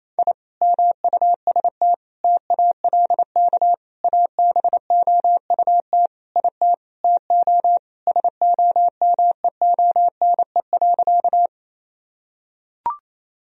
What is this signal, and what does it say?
Morse code, heard as a keyed tone, about 28 wpm, 700 Hz. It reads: I MUST TALK ABOUT IT TO SOMEONE.
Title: Sentences from Top 1000 Words - Review Part 4 - 28wpm